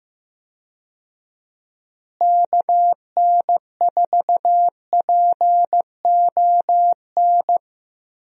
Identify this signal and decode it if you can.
Morse code, heard as a keyed tone, about 15 wpm, 700 Hz. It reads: KN4PON
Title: USA Call Signs - 6 Characters - Repeated 3x - 15wpm